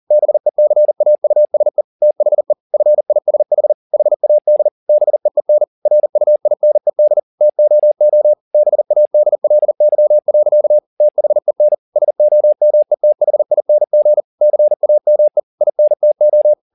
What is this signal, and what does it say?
Morse code, heard as a keyed tone, about 40 wpm, 600 Hz. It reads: BECAUSE THE FISH HAD BEEN RUINED TOO BADLY. THEN SOMETHING CAME INTO
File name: the old man and the sea 2 40wpm